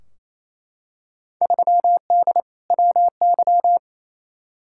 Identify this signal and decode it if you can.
Morse code, heard as a keyed tone, about 28 wpm, 700 Hz. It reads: 3D WY
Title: Field Day Exchanges Encore - 28wpm